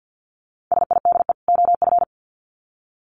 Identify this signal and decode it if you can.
Morse code, heard as a keyed tone, about 50 wpm, 700 Hz. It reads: SIDE OF
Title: Sets of 2 Words 50wpm